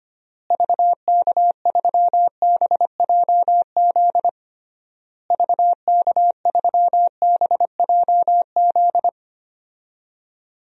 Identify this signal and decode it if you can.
Morse code, heard as a keyed tone, about 25 wpm, 700 Hz. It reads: VK3BJZ VK3BJZ
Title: DXCC-100 Call Signs - 6 Characters - Repeated 3x - 25wpm